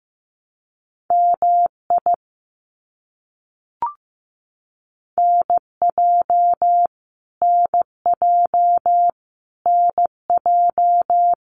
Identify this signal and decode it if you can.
Morse code, heard as a keyed tone, about 15 wpm, 700 Hz. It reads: MI NJ NJ NJ
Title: US States Repeated 3x 15wpm